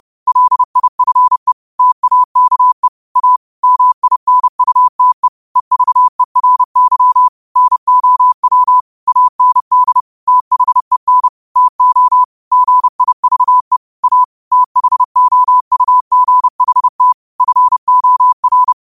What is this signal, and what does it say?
Morse code, heard as a keyed tone, about 30 wpm, 1000 Hz. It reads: LIFE TAKE A MINUTE EVERY NOW AND THEN TO GIVE A THOUGHT FOR